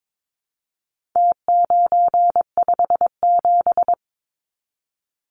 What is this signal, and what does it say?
Morse code, heard as a keyed tone, about 22 wpm, 700 Hz. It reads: T957
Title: Numbers - Contest Style - 4-Digit 22wpm